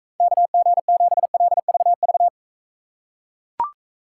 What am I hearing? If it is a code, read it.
Morse code, heard as a keyed tone, about 42 wpm, 700 Hz. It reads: KG7LVV